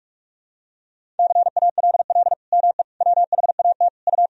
Morse code, heard as a keyed tone, about 46 wpm, 700 Hz. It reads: CALL ME WHAT U
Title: Sentences from Top 200 Words and Abbreviations 45wpm